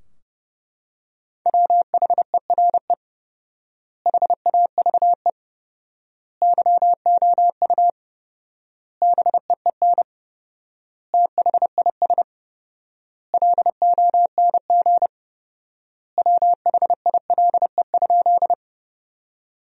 Morse code, heard as a keyed tone, about 30 wpm, 700 Hz. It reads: WHERE HAVE YOU BEEN THIS LONG WHILE?
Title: 4x Word-Spacing - Sentences from Top 300 Words - 30wpm